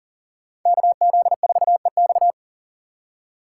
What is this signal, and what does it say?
Morse code, heard as a keyed tone, about 49 wpm, 700 Hz. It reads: KZ4EX
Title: Speed Racing - USA 2x2 Call Signs 40wpm